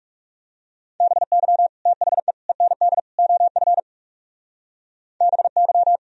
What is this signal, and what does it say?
Morse code, heard as a keyed tone, about 37 wpm, 700 Hz. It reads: BY THE END OF BY
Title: Speed-Racing - Sets of 4 Words 30wpm